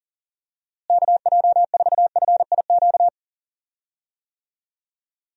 Morse code, heard as a keyed tone, about 40 wpm, 700 Hz. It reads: KJ4FIQ
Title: USA Call Signs - 6 Characters - Repeated 3x - 40wpm